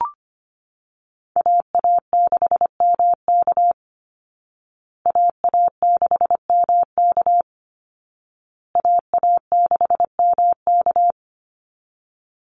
Morse code, heard as a keyed tone, about 25 wpm, 700 Hz. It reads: AA6MK AA6MK AA6MK
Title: Call Signs Repeated 2x - 25wpm